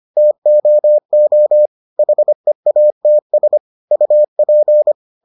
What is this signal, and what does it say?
Morse code, heard as a keyed tone, about 25 wpm, 600 Hz. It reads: TOO HEATS UP